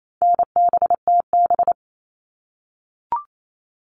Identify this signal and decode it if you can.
Morse code, heard as a keyed tone, about 28 wpm, 700 Hz. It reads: NBTB
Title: All Cut Numbers - 4-Digit 28wpm